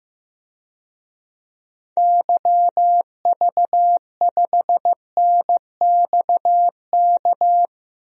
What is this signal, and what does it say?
Morse code, heard as a keyed tone, about 15 wpm, 700 Hz. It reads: YV5NXK
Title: DXCC-100 Call Signs - 6 Characters - Repeated 2x - 15wpm